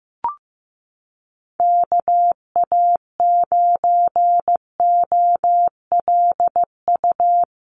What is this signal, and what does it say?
Morse code, heard as a keyed tone, about 15 wpm, 700 Hz. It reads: KA9OLU
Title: USA Call Signs Repeated 2x 15wpm